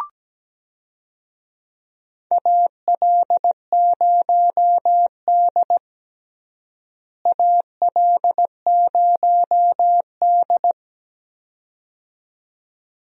Morse code, heard as a keyed tone, about 17 wpm, 700 Hz. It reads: AL0D AL0D
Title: USA Call Signs - 4 Characters - Repeated 2x - 17wpm